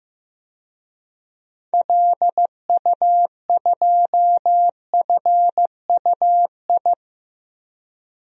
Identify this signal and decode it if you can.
Morse code, heard as a keyed tone, about 16 wpm, 700 Hz. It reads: LU2FUI
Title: DXCC-100 Call Signs - 6 Characters - 15wpm